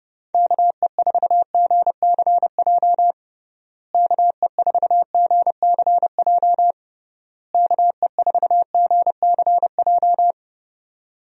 Morse code, heard as a keyed tone, about 30 wpm, 700 Hz. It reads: KE4GCJ KE4GCJ KE4GCJ
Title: USA Call Signs - 6 Characters - Repeated 3x - 30wpm